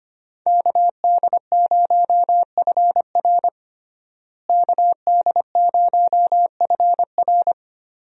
Morse code, heard as a keyed tone, about 25 wpm, 700 Hz. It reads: KD0FR KD0FR
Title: USA Call Signs - 5 Characters - Repeated 3x - 25wpm